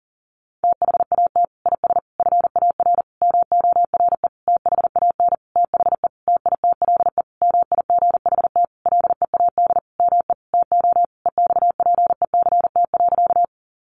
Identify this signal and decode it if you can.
Morse code, heard as a keyed tone, about 37 wpm, 700 Hz. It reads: THAT IS FAR MORE THAN THE TITLE MIGHT LEAD ME TO EXPECT.
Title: Sentences from Top 1200 Words 40wpm